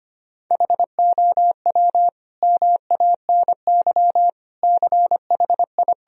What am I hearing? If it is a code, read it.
Morse code, heard as a keyed tone, about 25 wpm, 700 Hz. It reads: HOW MANY CHI
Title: Sentences from Top 1000 Words - Review Part 1 - 25wpm